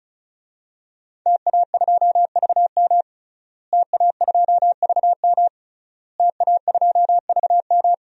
Morse code, heard as a keyed tone, about 36 wpm, 700 Hz. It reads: TA2VM TA2VM TA2VM
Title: DXCC-100 Call Signs - 5 Characters - Repeated 3x - 35wpm